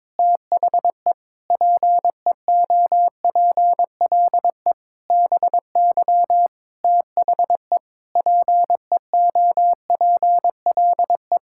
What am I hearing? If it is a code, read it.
Morse code, heard as a keyed tone, about 22 wpm, 700 Hz. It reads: THE PEOPLE BY THE PEOPLE
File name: Sets of 9 Words - 22wpm